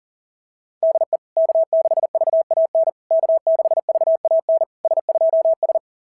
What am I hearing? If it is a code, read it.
Morse code, heard as a keyed tone, about 40 wpm, 660 Hz. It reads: DE K6VAN K6VAN S2S
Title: SOTA QSO Practice 40wpm